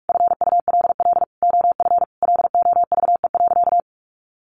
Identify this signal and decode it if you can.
Morse code, heard as a keyed tone, about 44 wpm, 700 Hz. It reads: FULL OF LOVE.
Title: Sentences from Top 1500 Words 45wpm